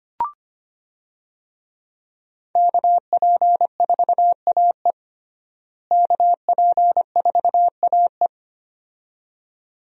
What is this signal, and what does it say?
Morse code, heard as a keyed tone, about 25 wpm, 700 Hz. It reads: KP4AE KP4AE
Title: USA Call Signs - 5 Characters - Repeated 2x - 25wpm